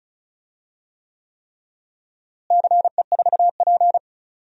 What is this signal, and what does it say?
Morse code, heard as a keyed tone, about 36 wpm, 700 Hz. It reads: CE4P
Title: DXCC-100 Call Signs - 4 Characters - Repeated 3x - 35wpm